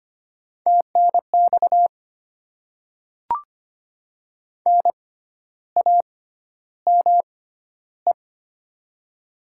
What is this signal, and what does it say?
Morse code, heard as a keyed tone, about 25 wpm, spaced as for 12 wpm, 700 Hz. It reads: TNX NAME
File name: Top 20 QSO Elements - 6x Character Spacing - 25wpm